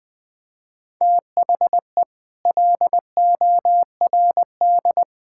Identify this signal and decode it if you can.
Morse code, heard as a keyed tone, about 20 wpm, 700 Hz. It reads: THE LORD
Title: Sets of 9 Words - 20wpm